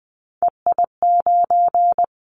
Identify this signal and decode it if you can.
Morse code, heard as a keyed tone, about 20 wpm, 700 Hz. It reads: EI9
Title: Callsign Prefixes for Top 50 DXCC - 20wpm